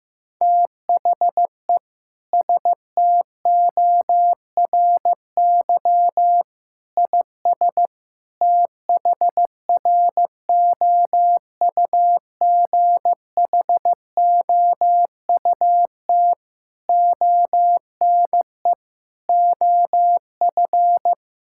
Morse code, heard as a keyed tone, about 15 wpm, 700 Hz. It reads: THE STORY IS THROUGHOUT ONE OF
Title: Sentences from Top 1000 Words - Review Part 5 - 15wpm